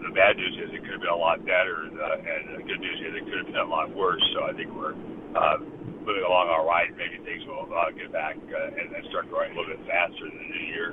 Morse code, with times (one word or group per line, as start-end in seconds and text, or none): none